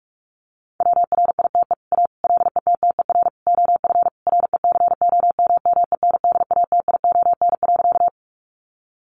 Morse code, heard as a keyed tone, about 44 wpm, 700 Hz. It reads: WRITE A LETTER OF RECOMMENDATION.
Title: Sentences from Top 1400 Words 45wpm